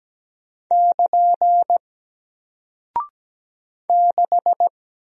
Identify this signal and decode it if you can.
Morse code, heard as a keyed tone, about 17 wpm, 700 Hz. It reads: <KN> 6
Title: Single Character (Letter:Number:Punctuation:Prosign) Alt 17wpm